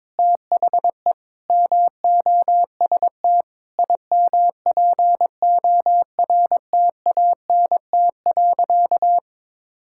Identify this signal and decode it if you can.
Morse code, heard as a keyed tone, about 22 wpm, 700 Hz. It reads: THE MOST IMPORTANT.
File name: practice / 22wpm / Sentences with 8-Word Phrases - 22wpm